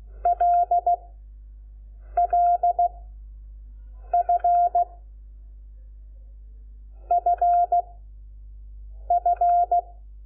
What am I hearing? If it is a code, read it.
Morse code, heard as a keyed tone, about 16 wpm, 700 Hz. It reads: L L F F F
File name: Lecciones / CLASE Nº5